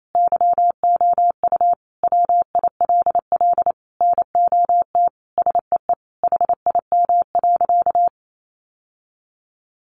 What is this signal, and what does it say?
Morse code, heard as a keyed tone, about 28 wpm, 700 Hz. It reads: YOU WILL NOT SEE HIM.
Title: Sentences from Top 600 Words 28wpm